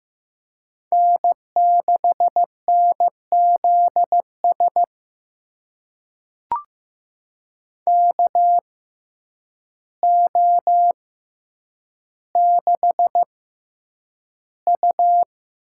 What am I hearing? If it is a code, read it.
Morse code, heard as a keyed tone, about 15 wpm, spaced as for 8 wpm, 700 Hz. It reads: N6NZS KO6U
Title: USA Call Signs - 5 Characters - 6x Character Spacing - 15wpm